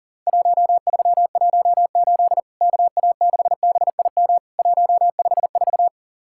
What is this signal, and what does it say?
Morse code, heard as a keyed tone, about 40 wpm, 700 Hz. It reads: 1318 KA6BIM 154
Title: Worked All Europe DX Contest - QTC Exchange - 40wpm